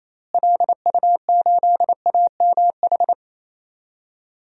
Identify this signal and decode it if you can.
Morse code, heard as a keyed tone, about 28 wpm, 700 Hz. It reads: LU8AMH
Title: DXCC-100 Call Signs - 6 Characters - Repeated 2x - 28wpm